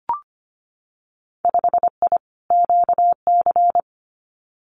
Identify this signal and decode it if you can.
Morse code, heard as a keyed tone, about 25 wpm, 700 Hz. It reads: 5I QC
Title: Winter Field Day Exchanges Encore 25wpm